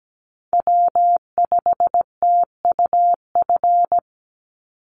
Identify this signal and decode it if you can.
Morse code, heard as a keyed tone, about 17 wpm, 700 Hz. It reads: W5TUF
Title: Call Signs Repeated 3x - 17wpm